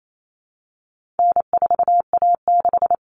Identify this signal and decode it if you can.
Morse code, heard as a keyed tone, about 28 wpm, 700 Hz. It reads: N4A6
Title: Common Cut Numbers - 4-Digit 28wpm